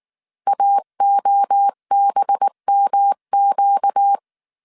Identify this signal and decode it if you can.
Morse code, heard as a keyed tone, about 19 wpm, 800 Hz. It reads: AOBMQ